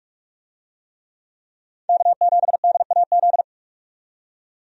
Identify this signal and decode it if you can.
Morse code, heard as a keyed tone, about 46 wpm, 700 Hz. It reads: K7DAZ
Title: USA Call Signs - 5 Characters - 45wpm